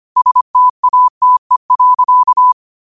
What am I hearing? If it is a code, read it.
Morse code, heard as a keyed tone, about 25 wpm, 1000 Hz. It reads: STATE.
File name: qotdfiles / 25wpm